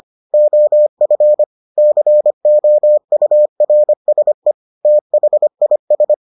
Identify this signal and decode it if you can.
Morse code, heard as a keyed tone, about 25 wpm, 600 Hz. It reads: OF COURSE THIS